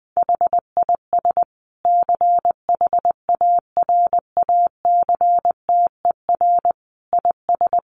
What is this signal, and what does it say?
Morse code, heard as a keyed tone, about 20 wpm, 700 Hz. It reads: HIS CHARACTER IS
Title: Sentences from Top 1200 Words 20wpm